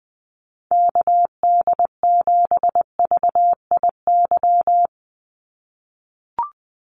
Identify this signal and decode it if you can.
Morse code, heard as a keyed tone, about 20 wpm, 700 Hz. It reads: KD7VIY
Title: USA Call Signs 20wpm